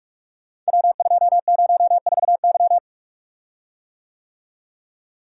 Morse code, heard as a keyed tone, about 44 wpm, 700 Hz. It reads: W204Y
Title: Code Groups - 5 Letter:Numbers - 45wpm